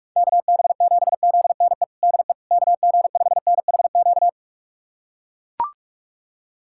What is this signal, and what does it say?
Morse code, heard as a keyed tone, about 41 wpm, 700 Hz. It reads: KB7ZNE DE KG5NHQ
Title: USA Callsign DE Callsign 45wpm